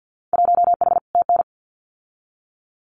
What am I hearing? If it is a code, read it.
Morse code, heard as a keyed tone, about 50 wpm, 700 Hz. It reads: 1H TN
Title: Winter Field Day Exchanges 50wpm